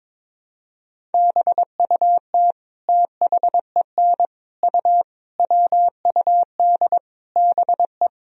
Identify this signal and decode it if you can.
Morse code, heard as a keyed tone, about 22 wpm, 700 Hz. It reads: BUT THEN U WUD BE
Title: Sentences from Top 300 Words and Abbreviations 22wpm